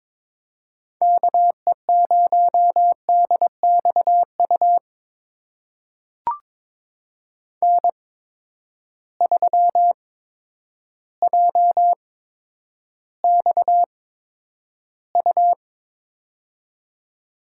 Morse code, heard as a keyed tone, about 22 wpm, spaced as for 10 wpm, 700 Hz. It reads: KE0DXU N3JXU
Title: USA Call Signs - 8x Character Spacing - 22wpm